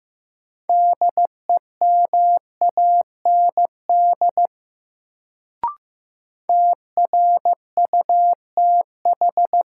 Morse code, heard as a keyed tone, about 15 wpm, 700 Hz. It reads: DEMAND TRUTH
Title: Sentences from Top 800 Words - New Word Drill - 15wpm